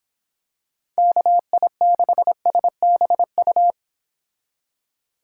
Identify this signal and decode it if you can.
Morse code, heard as a keyed tone, about 26 wpm, 700 Hz. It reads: KI6SBU